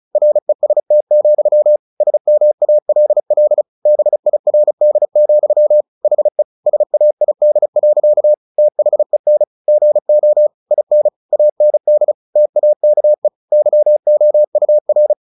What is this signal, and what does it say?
Morse code, heard as a keyed tone, about 34 wpm, 600 Hz. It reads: REST, SMALL BIRD, HE SAID. THEN GO IN AND TAKE YOUR
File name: the old man and the sea / the old man and the sea 1 35wpm